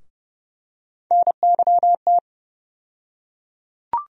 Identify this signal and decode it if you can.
Morse code, heard as a keyed tone, about 31 wpm, 700 Hz. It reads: NYT
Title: USA Call Sign Postfixes - 3 Letters 30wpm